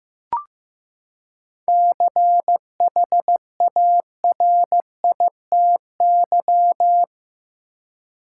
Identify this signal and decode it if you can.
Morse code, heard as a keyed tone, about 15 wpm, 700 Hz. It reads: CHARITY
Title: Seven Letter Words - 15wpm